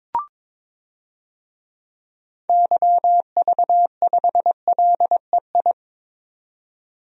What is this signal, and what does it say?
Morse code, heard as a keyed tone, about 22 wpm, 700 Hz. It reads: YV5LEI
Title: DXCC-100 Call Signs - 6 Characters - 22wpm